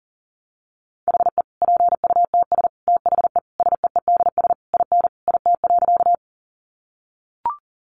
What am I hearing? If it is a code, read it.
Morse code, heard as a keyed tone, about 40 wpm, 700 Hz. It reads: HE PUTS THE SEEDS IN IT.
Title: Easy Phrases 40wpm